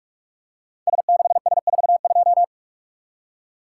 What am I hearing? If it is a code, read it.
Morse code, heard as a keyed tone, about 44 wpm, 700 Hz. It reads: S6S42
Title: Code Groups - 5 Letter:Numbers - 45wpm